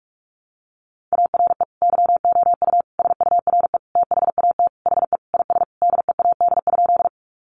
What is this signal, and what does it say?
Morse code, heard as a keyed tone, about 44 wpm, 700 Hz. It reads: ARE YOU SURE THAT HE IS DEAD?
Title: Sentences from Top 1500 Words 45wpm